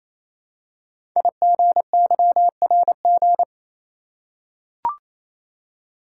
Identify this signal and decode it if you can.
Morse code, heard as a keyed tone, about 27 wpm, 700 Hz. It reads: IGYRG
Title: Code Groups - 5 Letters - 28wpm